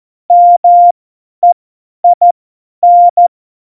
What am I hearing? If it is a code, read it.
Morse code, heard as a keyed tone, about 14 wpm, spaced as for 10 wpm, 700 Hz. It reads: MEIN
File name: Morsekurs / Witze / Witz 02. Juli 2019, 50 F70